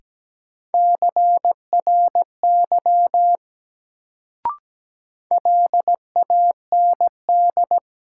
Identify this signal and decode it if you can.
Morse code, heard as a keyed tone, about 17 wpm, 700 Hz. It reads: CRY LAND
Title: Sentences from Top 500 Words - New Word Drill - 17wpm